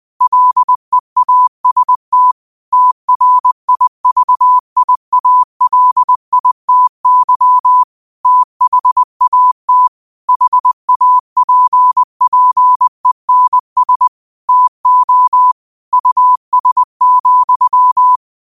Morse code, heard as a keyed tone, about 20 wpm, 1000 Hz. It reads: LEAST TRIVIALITY THAT HAPPENS TO US,
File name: qotdfiles / 20wpm